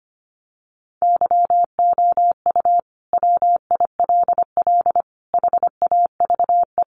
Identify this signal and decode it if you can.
Morse code, heard as a keyed tone, about 25 wpm, 700 Hz. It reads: YOU WILL HAVE
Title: Sentences from Top 1100 Words 25wpm